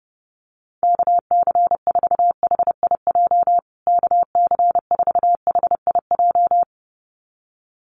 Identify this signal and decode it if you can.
Morse code, heard as a keyed tone, about 30 wpm, 700 Hz. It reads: KC4HIJ KC4HIJ
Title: USA Call Signs Repeated 2x 30wpm